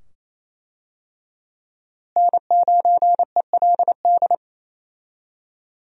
Morse code, heard as a keyed tone, about 28 wpm, 700 Hz. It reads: N9ELD
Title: USA Call Signs - 5 Characters - Repeated 2x - 28wpm